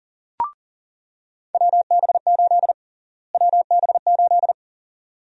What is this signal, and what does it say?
Morse code, heard as a keyed tone, about 40 wpm, 700 Hz. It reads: WB8 WB8
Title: USA Call Sign Prefixes - Repeated 2x - 40wpm